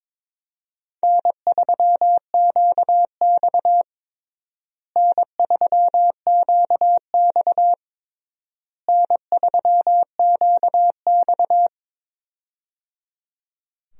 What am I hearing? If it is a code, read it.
Morse code, heard as a keyed tone, about 22 wpm, 700 Hz. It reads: N3QX N3QX N3QX
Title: USA Call Signs - 4 Characters - Repeated 3x - 22wpm